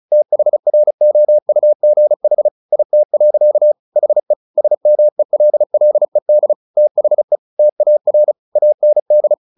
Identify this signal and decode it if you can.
Morse code, heard as a keyed tone, about 34 wpm, 600 Hz. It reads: THROUGH IT. HE SMELLED THE TAR AND
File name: the old man and the sea / the old man and the sea 1 35wpm